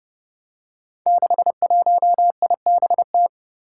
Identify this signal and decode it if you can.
Morse code, heard as a keyed tone, about 30 wpm, 700 Hz. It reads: 61IBT
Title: Five Challenging Letter-Number Combinations 30wpm